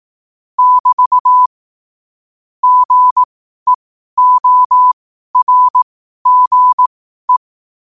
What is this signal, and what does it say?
Morse code, heard as a keyed tone, about 18 wpm, spaced as for 12 wpm, 1000 Hz. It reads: <BT> GEORGE
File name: qotdfiles / 12wpm